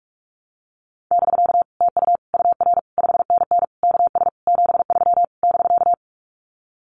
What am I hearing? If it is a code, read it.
Morse code, heard as a keyed tone, about 44 wpm, 700 Hz. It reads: <BK> TU UR 5NN KS 73 <BK>
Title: Common QSO Phrases Encore 45wpm